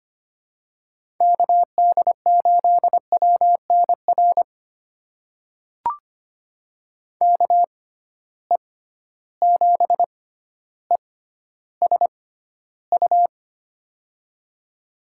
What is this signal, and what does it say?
Morse code, heard as a keyed tone, about 25 wpm, spaced as for 13 wpm, 700 Hz. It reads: KD8WNR KE7ESU
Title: USA Call Signs - 6 Characters - 6x Character Spacing - 25wpm